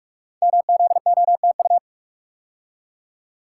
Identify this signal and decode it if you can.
Morse code, heard as a keyed tone, about 44 wpm, 700 Hz. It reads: MZOTU
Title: Code Groups - 5 Letters - 45wpm